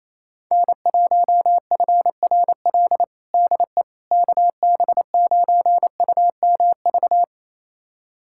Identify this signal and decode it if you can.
Morse code, heard as a keyed tone, about 27 wpm, 700 Hz. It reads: N1FRL DE KB9UMV
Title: USA Callsign DE Callsign 28wpm